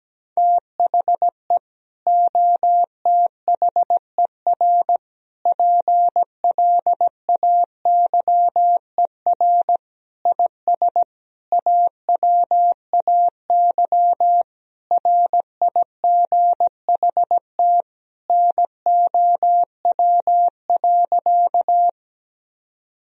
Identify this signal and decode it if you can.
Morse code, heard as a keyed tone, about 17 wpm, 700 Hz. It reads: THE OTHER PLAYER IS AWAY RIGHT NOW.
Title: Sentences from Top 1000 Words - Review Part 4 - 17wpm